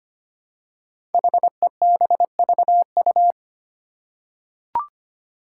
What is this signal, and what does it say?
Morse code, heard as a keyed tone, about 25 wpm, 700 Hz. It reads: HEBVU
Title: Five Challenging Letter-Number Combinations 25wpm